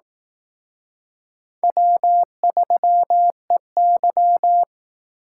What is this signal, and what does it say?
Morse code, heard as a keyed tone, about 19 wpm, 700 Hz. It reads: W3EY
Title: Speed Racing - USA 1x2 Call Signs 15wpm